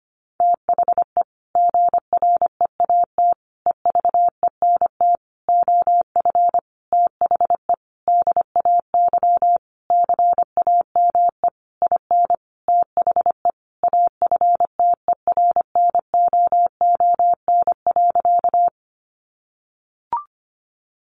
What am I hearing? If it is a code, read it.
Morse code, heard as a keyed tone, about 25 wpm, 700 Hz. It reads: THE GREAT EVENT OF THE DAY CAME IN THE AFTERNOON.
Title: Sentences from Top 1500 Words 25wpm